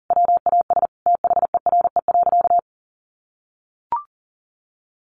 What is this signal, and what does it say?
Morse code, heard as a keyed tone, about 40 wpm, 700 Hz. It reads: WAS THERE.
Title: Sentences from Top 500 Words 40wpm